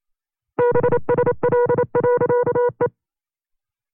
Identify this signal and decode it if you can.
Morse code, heard as a keyed tone, about 27 wpm, 490 Hz. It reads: BSL.E
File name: CW-practice/25WPM